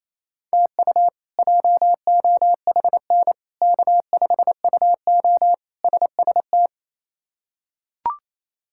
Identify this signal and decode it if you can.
Morse code, heard as a keyed tone, about 28 wpm, 700 Hz. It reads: TU JOHN K5UO SST
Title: K1USN SST QSO Practice - 28wpm